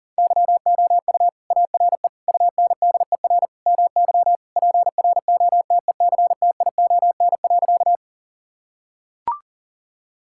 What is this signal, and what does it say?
Morse code, heard as a keyed tone, about 40 wpm, 700 Hz. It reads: YOU ARE UNDER MY PROTECTION.
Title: Sentences from Top 1600 Words 40wpm